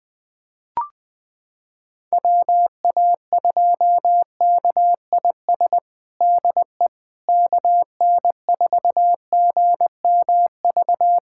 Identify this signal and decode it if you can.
Morse code, heard as a keyed tone, about 21 wpm, 700 Hz. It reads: WA2KIS DE KN4GMV
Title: USA Callsign DE Callsign 20wpm